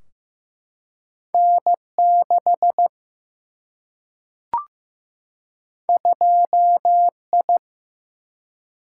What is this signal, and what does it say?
Morse code, heard as a keyed tone, about 15 wpm, 700 Hz. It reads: N 6 2 I
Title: Single-Letter Single-Number 15wpm